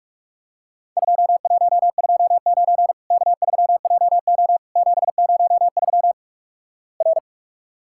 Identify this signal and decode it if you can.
Morse code, heard as a keyed tone, about 47 wpm, 700 Hz. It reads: 2129 K3JO 703 R
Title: Worked All Europe DX Contest - QTC Exchange - 45wpm